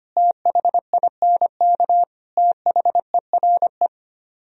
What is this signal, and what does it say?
Morse code, heard as a keyed tone, about 31 wpm, 700 Hz. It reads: THINK THERE
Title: Speed-Racing - Sets of 3 Words 25wpm